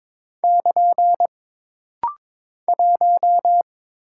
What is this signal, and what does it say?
Morse code, heard as a keyed tone, about 22 wpm, 700 Hz. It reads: <KN> 1
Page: This practice set is composed of Letters, Numbers, common punctuations, and common prosigns.